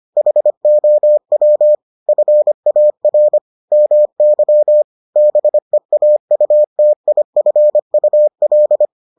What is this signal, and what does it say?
Morse code, heard as a keyed tone, about 25 wpm, 600 Hz. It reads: HOW FAR MY BEAUTIFUL